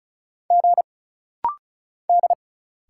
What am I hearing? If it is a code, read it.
Morse code, heard as a keyed tone, about 36 wpm, 700 Hz. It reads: G D
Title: Single Letter-Number - Rapid-Fire 35wpm